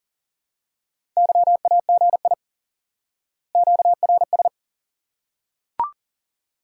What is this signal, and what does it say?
Morse code, heard as a keyed tone, about 40 wpm, 700 Hz. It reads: YAGI QRS
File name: IQE Course 3 - Lesson 89 - YAGI - Two Elements - 40wpm